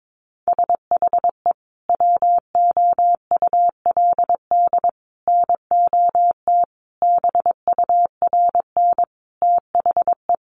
Morse code, heard as a keyed tone, about 22 wpm, 700 Hz. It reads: SHE WOULD NOT BURN THE
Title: Sentences from Top 1300 Words 22wpm